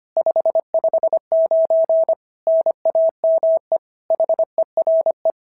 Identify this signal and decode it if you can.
Morse code, heard as a keyed tone, about 25 wpm, 690 Hz. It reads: 559 NAME HERE
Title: SOTA QSO Practice 25wpm